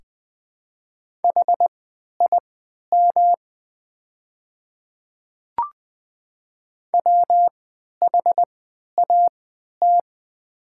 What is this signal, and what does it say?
Morse code, heard as a keyed tone, about 20 wpm, spaced as for 11 wpm, 700 Hz. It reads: HIM WHAT
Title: 3x Character Spacing - Top 100 Words - 20wpm